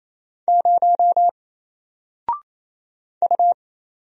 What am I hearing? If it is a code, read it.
Morse code, heard as a keyed tone, about 27 wpm, 700 Hz. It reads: 0 U
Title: Single Letter-Number 28wpm